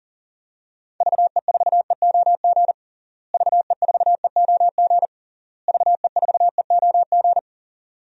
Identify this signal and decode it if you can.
Morse code, heard as a keyed tone, about 40 wpm, 700 Hz. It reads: VE4EOG VE4EOG VE4EOG
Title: DXCC-100 Call Signs - 6 Characters - Repeated 3x - 40wpm